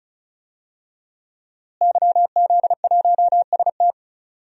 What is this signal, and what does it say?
Morse code, heard as a keyed tone, about 36 wpm, 700 Hz. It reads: YZ1ST
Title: DXCC-100 Call Signs - 5 Characters - Repeated 3x - 35wpm